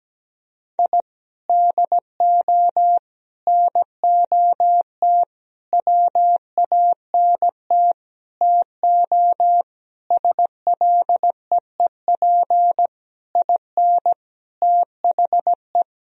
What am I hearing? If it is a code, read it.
Morse code, heard as a keyed tone, about 17 wpm, 700 Hz. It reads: I DO NOT WANT TO SLEEP IN THE
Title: Sentences from Top 1000 Words - Review Part 2 - 17wpm